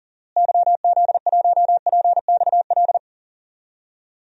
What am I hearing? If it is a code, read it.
Morse code, heard as a keyed tone, about 40 wpm, 700 Hz. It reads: YZ1PXL
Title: DXCC-100 Call Signs - 6 Characters - Repeated 3x - 40wpm